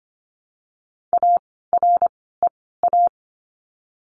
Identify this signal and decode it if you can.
Morse code, heard as a keyed tone, about 25 wpm, spaced as for 16 wpm, 700 Hz. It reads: AREA
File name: course / 25:10wpm / 094 Lesson 16f - Letter c - Words 25wpm